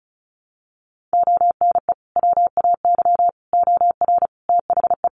Morse code, heard as a keyed tone, about 34 wpm, 700 Hz. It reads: ONE WAY OR THE
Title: Sets of 5 Words 35wpm